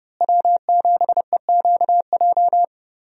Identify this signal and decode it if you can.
Morse code, heard as a keyed tone, about 31 wpm, 700 Hz. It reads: W7EQJ
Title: USA Call Signs - 5 Characters - Repeated 2x - 30wpm